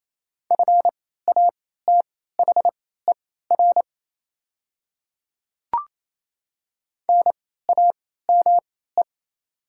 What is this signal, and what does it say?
Morse code, heard as a keyed tone, about 28 wpm, spaced as for 16 wpm, 700 Hz. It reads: FATHER NAME
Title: 3x Character Spacing - Top 200 Words - 28wpm